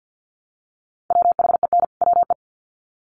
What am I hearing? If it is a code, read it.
Morse code, heard as a keyed tone, about 50 wpm, 700 Hz. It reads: WHEN WE
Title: Sets of 2 Words 50wpm